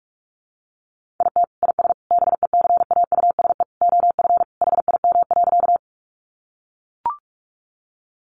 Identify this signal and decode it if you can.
Morse code, heard as a keyed tone, about 44 wpm, 700 Hz. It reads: IT IS BECAUSE OF HIM.
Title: Sentences from Top 600 Words 45wpm